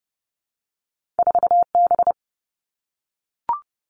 Morse code, heard as a keyed tone, about 30 wpm, 700 Hz. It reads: 4B
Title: All Cut Numbers - 2-Digit 30wpm